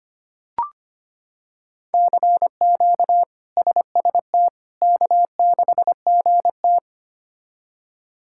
Morse code, heard as a keyed tone, about 25 wpm, 700 Hz. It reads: CQ SST K6GT
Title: K1USN SST QSO Practice - 25wpm